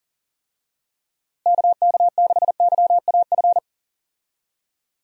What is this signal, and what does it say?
Morse code, heard as a keyed tone, about 40 wpm, 700 Hz. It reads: KK6YAF
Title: USA Call Signs - 6 Characters - Repeated 3x - 40wpm